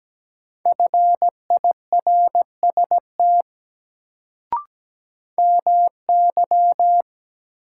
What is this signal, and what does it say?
Morse code, heard as a keyed tone, about 17 wpm, 700 Hz. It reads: FIRST MY
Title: Sets of 2 Words - New Word Drill - 17wpm